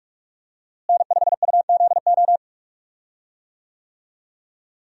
Morse code, heard as a keyed tone, about 47 wpm, 700 Hz. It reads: N5UZO
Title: USA Call Signs - 5 Characters - Repeated 2x - 45wpm